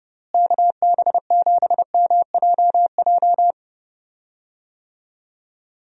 Morse code, heard as a keyed tone, about 31 wpm, 700 Hz. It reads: KB7MJJ
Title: USA Call Signs - 6 Characters - Repeated 3x - 30wpm